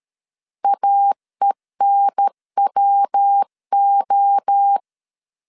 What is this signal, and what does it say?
Morse code, heard as a keyed tone, about 13 wpm, 800 Hz. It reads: AENWO
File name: CW / DET / 1 3 1500C S B 12wpm 2828